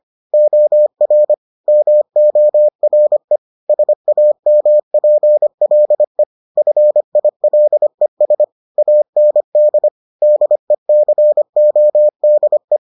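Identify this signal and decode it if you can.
Morse code, heard as a keyed tone, about 25 wpm, 600 Hz. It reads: OR MORE SAMPLE FILES AND DECODE
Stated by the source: Morse edition of the Foundations of Amateur Radio episodes.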